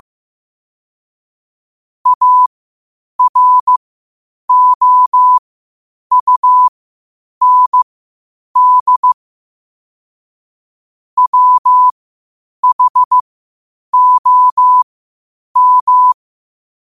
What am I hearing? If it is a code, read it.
Morse code, heard as a keyed tone, about 15 wpm, spaced as for 8 wpm, 1000 Hz. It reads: AROUND WHOM
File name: qotdfiles / 8wpm